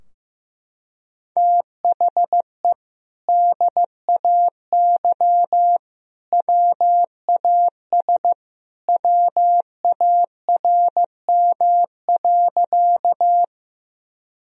Morse code, heard as a keyed tone, about 15 wpm, 700 Hz. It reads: THE DAY WAS WARM.
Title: Sentences from Top 1000 Words - Review Part 2 - 15wpm